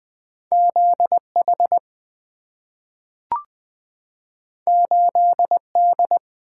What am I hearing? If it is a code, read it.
Morse code, heard as a keyed tone, about 20 wpm, 700 Hz. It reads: ZH 8D
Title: Two Letter-Number Combinations - 20wpm